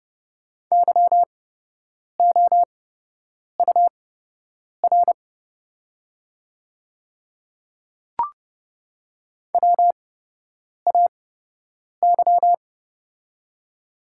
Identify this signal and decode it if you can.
Morse code, heard as a keyed tone, about 30 wpm, spaced as for 8 wpm, 700 Hz. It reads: YOUR WAY
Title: 8x Character Spacing - Top 100 Words - 30wpm